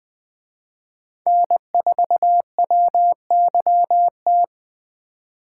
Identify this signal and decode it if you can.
Morse code, heard as a keyed tone, about 23 wpm, 700 Hz. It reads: N4WYT
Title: Speed Racing - USA 1x3 Call Signs 20wpm